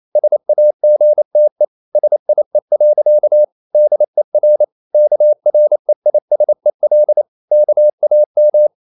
Morse code, heard as a keyed tone, about 28 wpm, 600 Hz. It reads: SAGTE SIE. DER KREISEL KAM